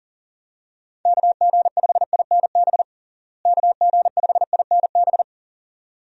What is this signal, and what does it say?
Morse code, heard as a keyed tone, about 40 wpm, 700 Hz. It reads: KG5INB KG5INB
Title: USA Call Signs - 6 Characters - Repeated 2x - 40wpm